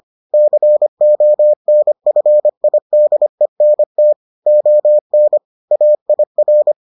Morse code, heard as a keyed tone, about 25 wpm, 600 Hz. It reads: CONFIDENT ON AIR